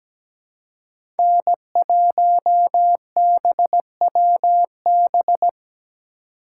Morse code, heard as a keyed tone, about 17 wpm, 700 Hz. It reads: N1BWB
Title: USA Call Signs - 5 Characters - 4x Character Spacing - 17wpm